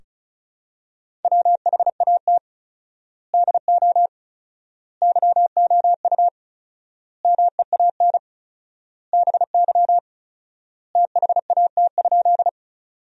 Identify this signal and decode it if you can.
Morse code, heard as a keyed tone, about 36 wpm, 700 Hz. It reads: WHAT DO YOU MEAN BY THAT?
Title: 4x Word-Spacing - Sentences from Top 200 Words - 35wpm